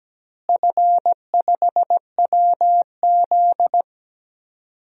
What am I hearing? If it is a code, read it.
Morse code, heard as a keyed tone, about 17 wpm, 700 Hz. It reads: F5WZ
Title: DXCC-100 Call Signs - 4 Characters - Repeated 3x - 17wpm